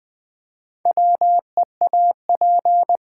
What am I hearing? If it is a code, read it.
Morse code, heard as a keyed tone, about 20 wpm, 700 Hz. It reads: WEAP
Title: Build A Word - 2 Letters At A Time - 6-Letter Words - 20wpm